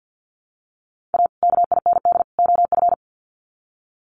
Morse code, heard as a keyed tone, about 48 wpm, 700 Hz. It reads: A KIND OF
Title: Sets of 3 Words 50wpm